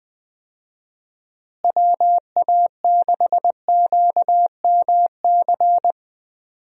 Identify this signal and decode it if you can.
Morse code, heard as a keyed tone, about 20 wpm, 700 Hz. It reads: WA6QMC
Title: USA Call Signs - 6 Characters - Repeated 2x - 20wpm